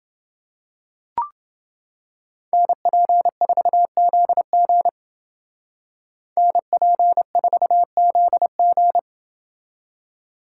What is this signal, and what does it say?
Morse code, heard as a keyed tone, about 29 wpm, 700 Hz. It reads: NP4ZG NP4ZG